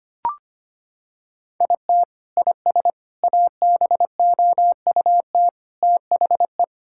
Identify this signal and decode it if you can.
Morse code, heard as a keyed tone, about 25 wpm, 700 Hz. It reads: IT IS ABOUT THE
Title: Sentences from Top 1000 Words - Review Part 6 - 25wpm